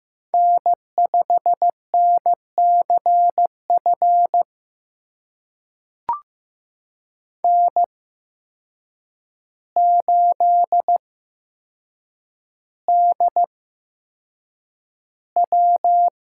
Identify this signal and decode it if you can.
Morse code, heard as a keyed tone, about 15 wpm, spaced as for 7 wpm, 700 Hz. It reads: N5NCF N8DW
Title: USA Call Signs - 5 Characters - 8x Character Spacing - 15wpm